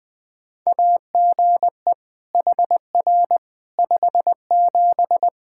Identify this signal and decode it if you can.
Morse code, heard as a keyed tone, about 20 wpm, 700 Hz. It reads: AGE HR 57
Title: Practice QSO - Exchange 3 - More Info - 20wpm